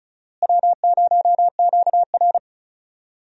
Morse code, heard as a keyed tone, about 34 wpm, 700 Hz. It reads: W0QR
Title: USA Call Signs - 4 Characters - Repeated 2x - 35wpm